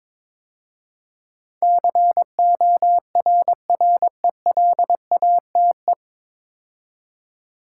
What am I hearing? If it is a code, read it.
Morse code, heard as a keyed tone, about 22 wpm, 700 Hz. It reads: CORRELATE
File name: 9 Letter Words - First 3 Letters Given - 22wpm